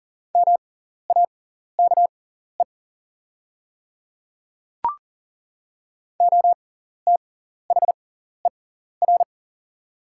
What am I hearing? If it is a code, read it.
Morse code, heard as a keyed tone, about 40 wpm, spaced as for 14 wpm, 690 Hz. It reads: MAKE OTHER
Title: 6x Character Spacing - Top 100 Words - 40wpm